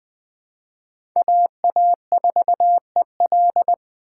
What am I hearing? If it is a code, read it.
Morse code, heard as a keyed tone, about 21 wpm, 700 Hz. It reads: AA4EL
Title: USA Call Signs - 5 Characters - 3x Character Spacing - 20wpm